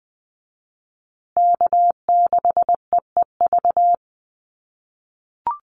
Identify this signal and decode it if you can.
Morse code, heard as a keyed tone, about 20 wpm, 700 Hz. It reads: K6EEV
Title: USA Call Signs Repeated 2x 20wpm